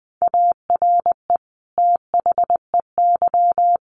Morse code, heard as a keyed tone, about 20 wpm, 700 Hz. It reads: ARE THEY
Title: Sentences from Top 1500 Words 20wpm